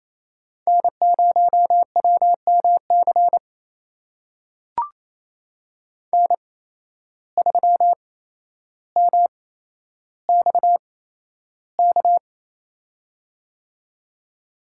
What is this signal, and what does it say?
Morse code, heard as a keyed tone, about 28 wpm, spaced as for 12 wpm, 700 Hz. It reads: N0WMC N3MXK
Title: USA Call Signs - 5 Characters - 8x Character Spacing - 28wpm